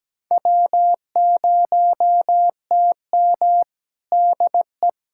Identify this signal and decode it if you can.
Morse code, heard as a keyed tone, about 17 wpm, 700 Hz. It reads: W0TM DE
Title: USA Callsign DE Callsign - 4 Characters - 17wpm